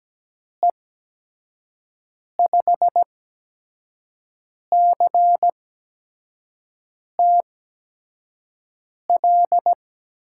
Morse code, heard as a keyed tone, about 17 wpm, spaced as for 5 wpm, 700 Hz. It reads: E5CTL
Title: USA Call Signs - 6 Characters - 8x Character Spacing - 17wpm